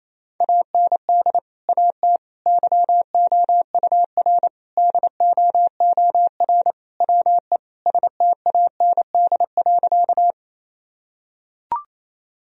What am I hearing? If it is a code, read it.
Morse code, heard as a keyed tone, about 28 wpm, 700 Hz. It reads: AND AT YOUR DOOR WE STAND.
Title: Sentences from Top 1000 Words - Review Part 1 - 28wpm